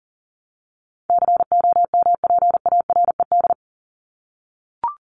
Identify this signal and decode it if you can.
Morse code, heard as a keyed tone, about 38 wpm, 700 Hz. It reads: COMPARED
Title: ED Suffix - Top 100 Words with Base Words 40wpm